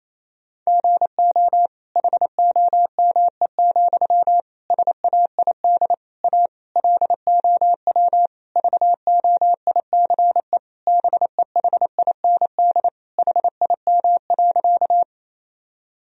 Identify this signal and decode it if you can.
Morse code, heard as a keyed tone, about 28 wpm, 700 Hz. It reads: GO HOME, SAID A LOW VOICE BEHIND HIM.
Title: Sentences from Top 1000 Words - Review Part 1 - 28wpm